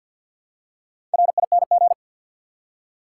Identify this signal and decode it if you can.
Morse code, heard as a keyed tone, about 48 wpm, 700 Hz. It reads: RING
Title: Four Letter Words - 50wpm